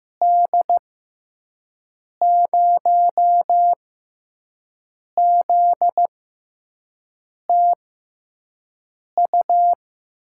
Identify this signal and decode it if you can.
Morse code, heard as a keyed tone, about 15 wpm, spaced as for 5 wpm, 700 Hz. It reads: D0ZTU
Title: USA Call Signs - 6 Characters - 6x Character Spacing - 15wpm